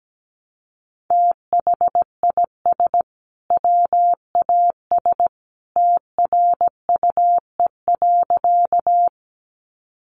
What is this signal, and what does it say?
Morse code, heard as a keyed tone, about 17 wpm, 700 Hz. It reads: THIS WAS TRUE.
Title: Sentences from Top 400 Words 17wpm